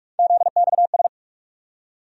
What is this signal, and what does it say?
Morse code, heard as a keyed tone, about 47 wpm, 700 Hz. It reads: ZXS
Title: USA Call Sign Postfixes - 3 Letters 45wpm